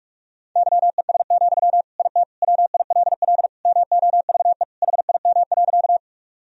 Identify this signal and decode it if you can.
Morse code, heard as a keyed tone, about 46 wpm, 700 Hz. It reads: YES, IT WILL MOVE HIM.
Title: Sentences from Top 1000 Words - Review Part 1 - 45wpm